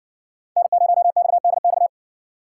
Morse code, heard as a keyed tone, about 58 wpm, 700 Hz. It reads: N0XDX